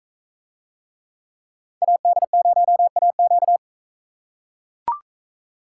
Morse code, heard as a keyed tone, about 40 wpm, 700 Hz. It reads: AD0AQ